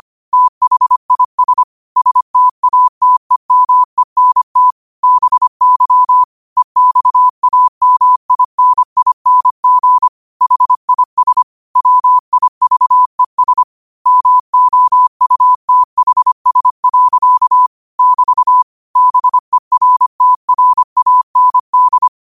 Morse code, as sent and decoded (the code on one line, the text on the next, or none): - .... .. ... / ... - .- - . -- . -. - / -... -.-- / . -..- .- -- .. -. .. -. --. / .... .. ... / .-- .. ...- . ... / -- --- ..- - .... ... .-.-.- / -...- / -... .-. - .-. .- -. -..
THIS STATEMENT BY EXAMINING HIS WIVES MOUTHS. <BT> BERTRAND